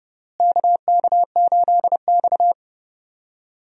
Morse code, heard as a keyed tone, about 32 wpm, 700 Hz. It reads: KK8X
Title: USA Call Signs - 4 Characters - Repeated 2x - 30wpm